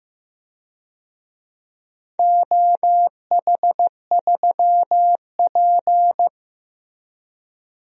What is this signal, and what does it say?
Morse code, heard as a keyed tone, about 15 wpm, 700 Hz. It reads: OH3P
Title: DXCC-100 Call Signs - 4 Characters - Repeated 2x - 15wpm